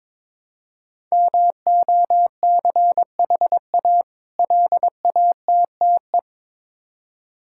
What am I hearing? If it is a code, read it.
Morse code, heard as a keyed tone, about 22 wpm, 690 Hz. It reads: MOCHA LATTE
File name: Difficult Words with Hints - 22wpm